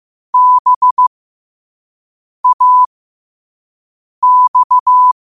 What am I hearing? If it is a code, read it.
Morse code, heard as a keyed tone, about 15 wpm, spaced as for 5 wpm, 1000 Hz. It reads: BAX